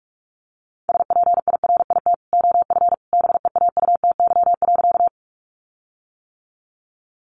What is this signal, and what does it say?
Morse code, heard as a keyed tone, about 44 wpm, 700 Hz. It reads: SPIRIT OF BEAUTY.
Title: Sentences from Top 1400 Words 45wpm